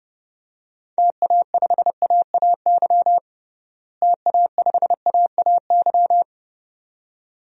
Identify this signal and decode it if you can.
Morse code, heard as a keyed tone, about 32 wpm, 700 Hz. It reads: TA5AAY TA5AAY
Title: DXCC-100 Call Signs - 6 Characters - Repeated 2x - 30wpm